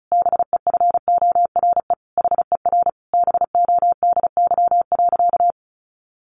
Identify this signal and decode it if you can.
Morse code, heard as a keyed tone, about 34 wpm, 700 Hz. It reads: BEFORE HER BODY.
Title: Sentences from Top 1500 Words 35wpm